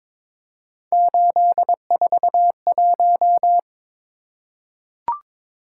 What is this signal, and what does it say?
Morse code, heard as a keyed tone, about 22 wpm, 700 Hz. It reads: 841
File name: practice / 22wpm / Three Letter-Number Combinations - 22wpm